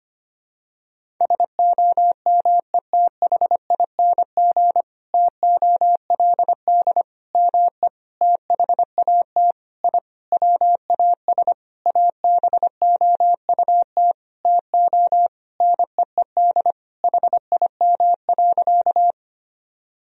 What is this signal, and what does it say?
Morse code, heard as a keyed tone, about 25 wpm, 700 Hz. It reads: SOMETHING TOLD ME THAT I WAS ABOUT TO NEED HIM.
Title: Sentences from Top 1000 Words - Review Part 1 - 25wpm